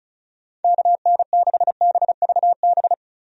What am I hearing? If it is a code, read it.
Morse code, heard as a keyed tone, about 36 wpm, 700 Hz. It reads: KN6BVB
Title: USA Call Signs - 6 Characters - 35wpm